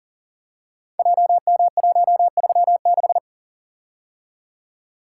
Morse code, heard as a keyed tone, about 42 wpm, 700 Hz. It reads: JM136
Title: Code Groups - 5 Letter:Numbers - 40wpm